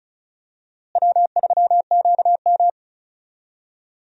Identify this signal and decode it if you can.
Morse code, heard as a keyed tone, about 34 wpm, 700 Hz. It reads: W3QM
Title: USA Call Signs - 4 Characters - Repeated 3x - 35wpm